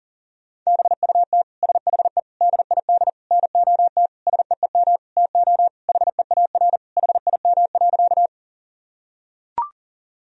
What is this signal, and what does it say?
Morse code, heard as a keyed tone, about 40 wpm, 700 Hz. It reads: BUT SHE DID NOT SEEM TO HEAR HIM.
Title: Sentences from Top 1000 Words - Review Part 1 - 40wpm